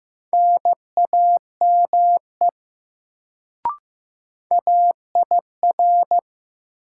Morse code, heard as a keyed tone, about 15 wpm, 700 Hz. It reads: NAME AIR
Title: Sentences from Top 200 Words - New Word Drill - 15wpm